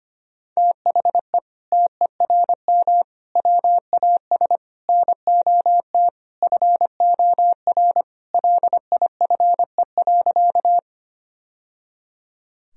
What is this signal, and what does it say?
Morse code, heard as a keyed tone, about 25 wpm, 700 Hz. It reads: THE TERM WAS NOT FOR LIFE.
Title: Sentences from Top 1000 Words - Review Part 4 - 25wpm